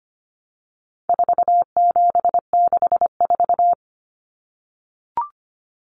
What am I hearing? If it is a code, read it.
Morse code, heard as a keyed tone, about 25 wpm, 700 Hz. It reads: 4764
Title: Numbers - 4-Digit 25wpm